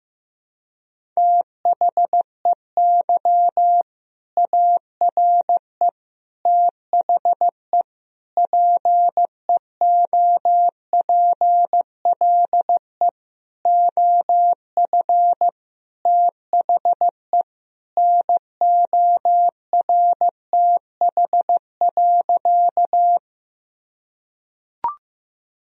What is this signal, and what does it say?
Morse code, heard as a keyed tone, about 15 wpm, 700 Hz. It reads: THEY ARE THE PEOPLE OF THE NORTH.
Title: Sentences from Top 1000 Words - Review Part 1 - 15wpm